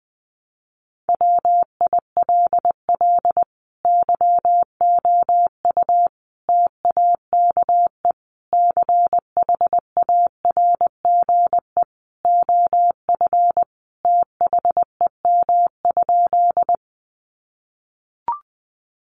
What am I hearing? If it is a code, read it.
Morse code, heard as a keyed tone, about 20 wpm, 700 Hz. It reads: WILL YOU TAKE CHARGE OF THEM?
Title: Sentences from Top 900 Words 20wpm